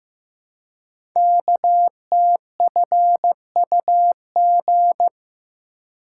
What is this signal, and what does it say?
Morse code, heard as a keyed tone, about 15 wpm, 700 Hz. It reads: KTFUG
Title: Code Groups - 5 Letters - 15wpm